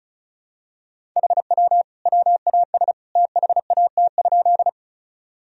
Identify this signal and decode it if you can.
Morse code, heard as a keyed tone, about 36 wpm, 700 Hz. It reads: HW WAS THAT?
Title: Sentences from Top 100 Words and Abbreviations 35wpm